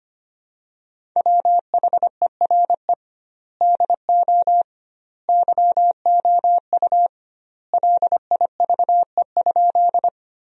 Morse code, heard as a keyed tone, about 25 wpm, 700 Hz. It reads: WHERE DO YOU LIVE?
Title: 2x Word-Spacing - Sentences from Top 200 Words - 25wpm